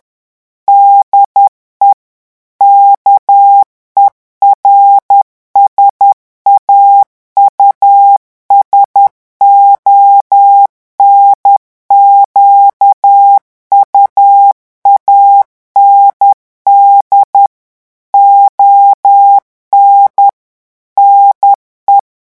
Morse code, heard as a keyed tone, about 10 wpm, 800 Hz. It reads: DE KERSAUSON QUAND ON NE
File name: revision9 citations humoristiques vitesse 10 mots minute